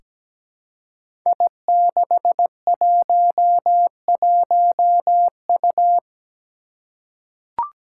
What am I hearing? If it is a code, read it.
Morse code, heard as a keyed tone, about 17 wpm, 700 Hz. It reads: I611U
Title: Five Challenging Letter-Number Combinations 17wpm